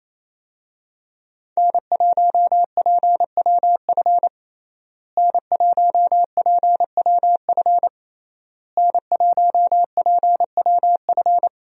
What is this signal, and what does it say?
Morse code, heard as a keyed tone, about 28 wpm, 700 Hz. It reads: N1PWF N1PWF N1PWF
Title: USA Call Signs - 5 Characters - Repeated 3x - 28wpm